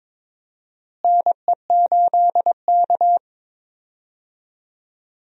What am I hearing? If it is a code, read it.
Morse code, heard as a keyed tone, about 22 wpm, 700 Hz. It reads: NE8K
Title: USA Call Signs - 4 Characters - Repeated 3x - 22wpm